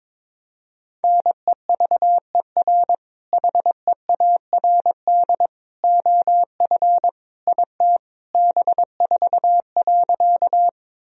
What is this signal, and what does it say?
Morse code, heard as a keyed tone, about 22 wpm, 700 Hz. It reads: NEVER HEARD OF IT B4.
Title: Sentences from Top 400 Words and Abbreviations 22wpm